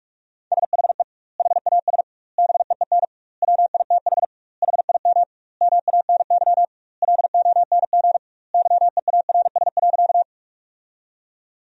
Morse code, heard as a keyed tone, about 42 wpm, 700 Hz. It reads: SHE HAS BEEN WITH HIM MANY LONG YEARS.
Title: 2x Word-Spacing - Sentences from Top 200 Words - 45wpm